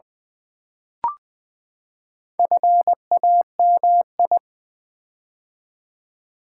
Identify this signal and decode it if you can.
Morse code, heard as a keyed tone, about 20 wpm, 700 Hz. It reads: FAMI
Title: Build A Word - 4 Letters At A Time - 8-Letter Words - 20wpm